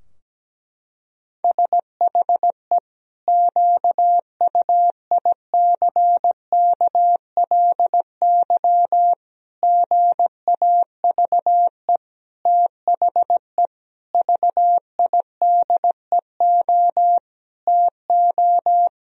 Morse code, heard as a keyed tone, about 17 wpm, 700 Hz. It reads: SHE QUICKLY GAVE THE VIDEO TO
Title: Sentences from Top 1000 Words - Review Part 4 - 17wpm